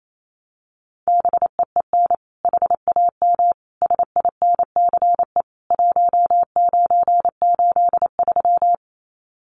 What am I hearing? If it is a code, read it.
Morse code, heard as a keyed tone, about 28 wpm, 700 Hz. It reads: BEEN HAM SINCE 1983
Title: Common QSO Phrases 28wpm